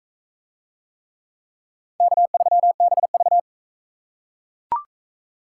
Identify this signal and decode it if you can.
Morse code, heard as a keyed tone, about 39 wpm, 700 Hz. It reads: K3BV